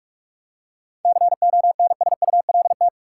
Morse code, heard as a keyed tone, about 45 wpm, 700 Hz. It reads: CONSULT
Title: Seven Letter Words - 45wpm